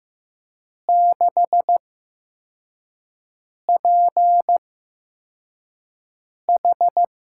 Morse code, heard as a keyed tone, about 15 wpm, spaced as for 4 wpm, 700 Hz. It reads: 6PH
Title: USA Call Signs - 8x Character Spacing - 15wpm